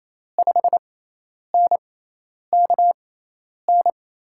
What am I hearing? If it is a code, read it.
Morse code, heard as a keyed tone, about 28 wpm, spaced as for 10 wpm, 700 Hz. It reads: 5NKN
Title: USA Call Signs - 6 Characters - 6x Character Spacing - 28wpm